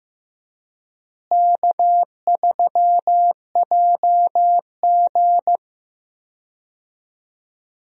Morse code, heard as a keyed tone, about 15 wpm, 700 Hz. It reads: K3JG
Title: USA Call Signs - 4 Characters - Repeated 3x - 15wpm